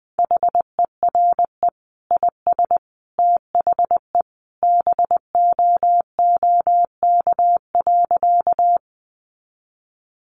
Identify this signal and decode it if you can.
Morse code, heard as a keyed tone, about 20 wpm, 700 Hz. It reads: HERE IS THE BOOK.
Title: Sentences from Top 300 Words 20wpm